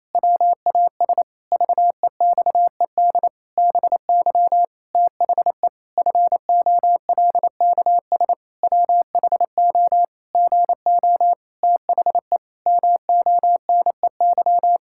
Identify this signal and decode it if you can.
Morse code, heard as a keyed tone, about 28 wpm, 700 Hz. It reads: WAS VEXED BY THE FOLKS WHO GO THE MONEY
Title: Pangram Challenge v2 - 28wpm